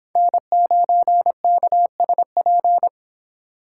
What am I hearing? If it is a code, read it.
Morse code, heard as a keyed tone, about 26 wpm, 700 Hz. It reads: N9KSP